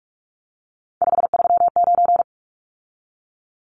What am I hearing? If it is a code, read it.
Morse code, heard as a keyed tone, about 44 wpm, 700 Hz. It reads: 539
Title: Numbers - 3-Digit 45wpm